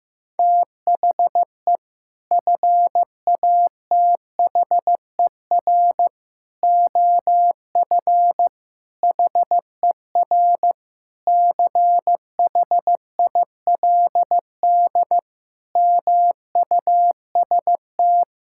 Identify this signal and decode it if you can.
Morse code, heard as a keyed tone, about 15 wpm, 700 Hz. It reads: THE FATHER OF HER CHILD MUST
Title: Sentences from Top 1000 Words - Review Part 5 - 15wpm